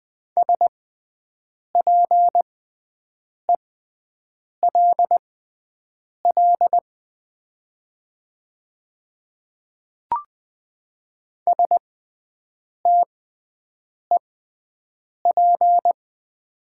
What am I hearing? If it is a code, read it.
Morse code, heard as a keyed tone, about 20 wpm, spaced as for 7 wpm, 700 Hz. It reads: SPELL STEP
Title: 6x Character Spacing - Top 500 Words - 20wpm